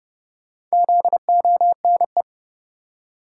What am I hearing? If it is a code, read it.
Morse code, heard as a keyed tone, about 32 wpm, 700 Hz. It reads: ZONE
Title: Four Letter Words - 30wpm